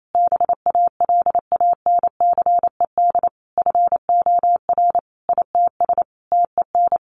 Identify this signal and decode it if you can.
Morse code, heard as a keyed tone, about 28 wpm, 700 Hz. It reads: BALANCED FOR ITS TEN
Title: A Sound Of Thunder 28wpm